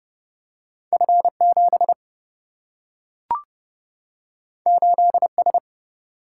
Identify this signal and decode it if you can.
Morse code, heard as a keyed tone, about 29 wpm, 700 Hz. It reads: F7 8S
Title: Single-Letter Single-Number 30wpm